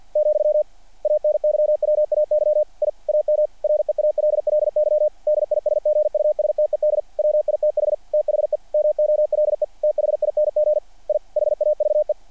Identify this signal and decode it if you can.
Morse code, heard as a keyed tone, about 48 wpm, 600 Hz. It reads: , ANYWAY I AM REALLY DISGUSTED WITH THE MOLE THING I HAVE
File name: archived / CRSnet